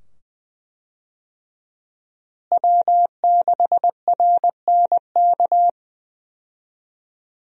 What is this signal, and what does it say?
Morse code, heard as a keyed tone, about 20 wpm, 700 Hz. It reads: W6RNK
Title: USA Call Signs - 5 Characters - Repeated 2x - 20wpm